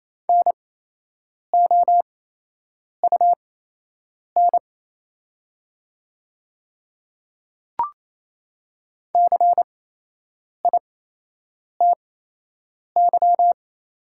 Morse code, heard as a keyed tone, about 28 wpm, spaced as for 8 wpm, 700 Hz. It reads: NOUN CITY
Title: 8x Character Spacing - Top 500 Words - 28wpm